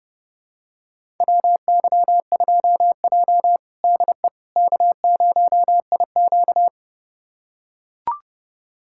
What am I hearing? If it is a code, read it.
Morse code, heard as a keyed tone, about 32 wpm, 700 Hz. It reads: WY2J DE K0IQ
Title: USA Callsign DE Callsign - 4 Characters - 30wpm